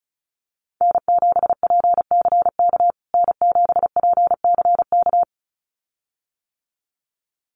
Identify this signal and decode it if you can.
Morse code, heard as a keyed tone, about 34 wpm, 700 Hz. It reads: N7PCK N7PCK
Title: USA Call Signs Repeated 2x 35wpm